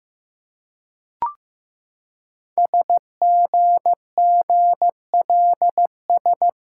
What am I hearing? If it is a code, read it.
Morse code, heard as a keyed tone, about 15 wpm, 700 Hz. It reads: SGGLS
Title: Code Groups - 5 Letters - 15wpm